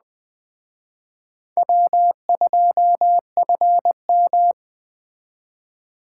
Morse code, heard as a keyed tone, about 19 wpm, 700 Hz. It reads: W2FM